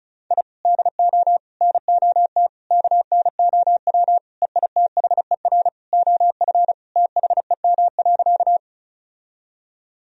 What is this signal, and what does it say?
Morse code, heard as a keyed tone, about 34 wpm, 700 Hz. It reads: I DO NOT KNOW EITHER OF THEM.
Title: Sentences from Top 1000 Words - Review Part 4 - 35wpm